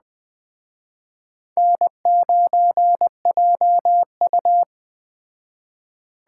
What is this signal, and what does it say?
Morse code, heard as a keyed tone, about 21 wpm, 700 Hz. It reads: N9JU
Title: Speed Racing - USA 1x2 Call Signs 17wpm